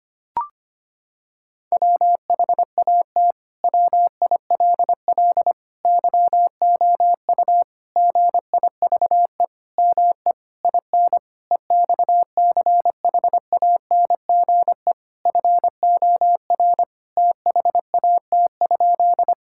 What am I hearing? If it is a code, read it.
Morse code, heard as a keyed tone, about 25 wpm, 700 Hz. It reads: WHAT WILL YOU GIVE ME IN EXCHANGE FOR THAT?
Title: Sentences from Top 1000 Words - Review Part 6 - 25wpm